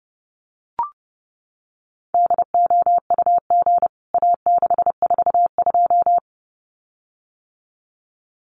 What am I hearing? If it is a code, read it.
Morse code, heard as a keyed tone, about 28 wpm, 700 Hz. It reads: DOUG A642
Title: CWT Exchange 30wpm